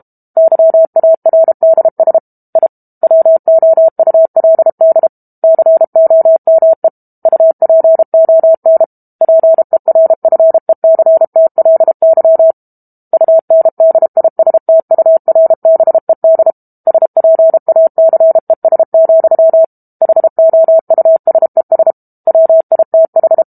Morse code, heard as a keyed tone, about 32 wpm, 650 Hz. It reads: YARDS I WOULD COME UPON PERFECTLY UNDISTURBED SPACES, HOUSES WITH
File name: Book Two / 06-Chapter 6 - 32 WPM